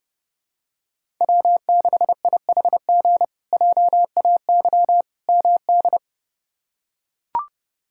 Morse code, heard as a keyed tone, about 31 wpm, 700 Hz. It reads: W6IHG JAY MD
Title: K1USN SST QSO Practice - 30wpm